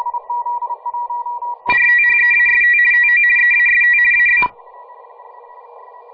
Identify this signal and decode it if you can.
RTTY, about 50 baud. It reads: RTHEAST TO EAST AB
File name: fr / DX RTTY